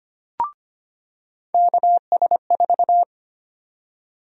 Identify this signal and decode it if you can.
Morse code, heard as a keyed tone, about 25 wpm, 700 Hz. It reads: KS4
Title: USA Call Sign Prefixes - 25wpm